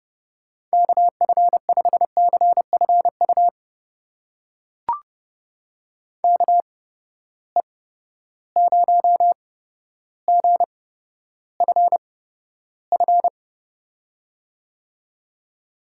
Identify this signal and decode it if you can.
Morse code, heard as a keyed tone, about 31 wpm, spaced as for 13 wpm, 700 Hz. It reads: KF5CFU KE0GFF
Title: USA Call Signs - 6 Characters - 8x Character Spacing - 30wpm